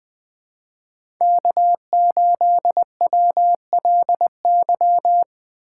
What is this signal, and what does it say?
Morse code, heard as a keyed tone, about 20 wpm, 700 Hz. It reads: K8WLY
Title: USA Call Signs - 5 Characters - 4x Character Spacing - 20wpm